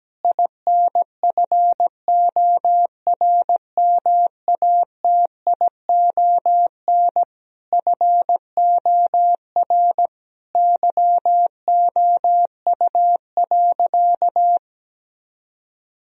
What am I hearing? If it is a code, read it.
Morse code, heard as a keyed tone, about 17 wpm, 700 Hz. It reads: INFORMATION FOR YOU.
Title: Sentences from Top 1000 Words - Review Part 3 - 17wpm